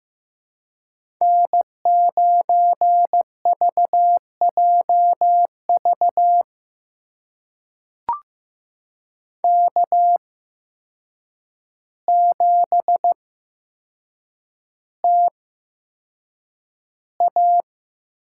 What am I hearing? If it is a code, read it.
Morse code, heard as a keyed tone, about 15 wpm, spaced as for 7 wpm, 700 Hz. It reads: N9VJV K7TA
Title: USA Call Signs - 5 Characters - 8x Character Spacing - 15wpm